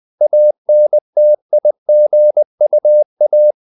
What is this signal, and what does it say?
Morse code, heard as a keyed tone, about 20 wpm, 600 Hz. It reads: ANTIGUA
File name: CW / Practice / NationalCapitals20